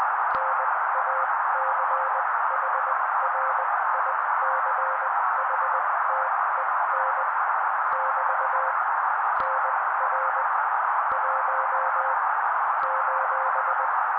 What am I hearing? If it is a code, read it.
Morse code, heard as a keyed tone, about 20 wpm, spaced as for 16 wpm, 530 Hz. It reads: NACHRICHTEN <BT> NR 1 :
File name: daily / 16wpm noise